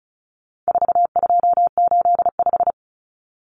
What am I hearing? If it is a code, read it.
Morse code, heard as a keyed tone, about 34 wpm, 700 Hz. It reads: 4285
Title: Common Cut Numbers - 4-Digit 35wpm